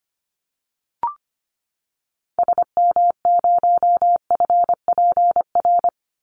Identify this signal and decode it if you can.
Morse code, heard as a keyed tone, about 25 wpm, 700 Hz. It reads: SM0FPR
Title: Call Signs Repeated 2x - 25wpm